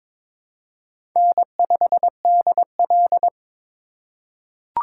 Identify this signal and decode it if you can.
Morse code, heard as a keyed tone, about 22 wpm, 700 Hz. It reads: N5DL
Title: USA Call Signs - 3x Character Spacing - 22wpm